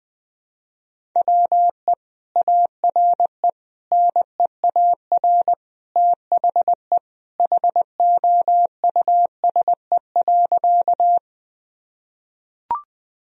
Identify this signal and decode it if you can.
Morse code, heard as a keyed tone, about 20 wpm, 700 Hz. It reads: WE ARE NEAR THE HOUSE.
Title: Sentences from Top 1000 Words - Review Part 1 - 20wpm